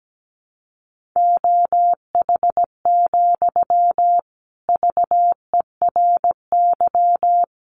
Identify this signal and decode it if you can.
Morse code, heard as a keyed tone, about 17 wpm, 700 Hz. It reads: OH, VERY
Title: Sentences from Top 900 Words 17wpm